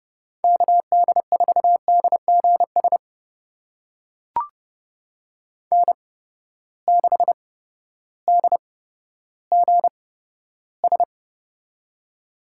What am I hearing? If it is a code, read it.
Morse code, heard as a keyed tone, about 31 wpm, spaced as for 14 wpm, 700 Hz. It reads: KD4DGS N6DGS
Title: USA Call Signs - 8x Character Spacing - 30wpm